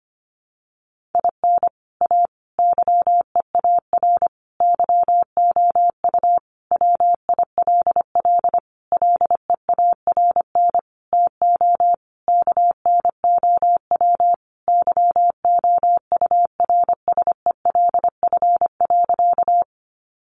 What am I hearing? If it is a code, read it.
Morse code, heard as a keyed tone, about 25 wpm, 700 Hz. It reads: IN A YEAR YOU WILL LEARN TO KNOW YOURSELF.
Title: Sentences from Top 600 Words 25wpm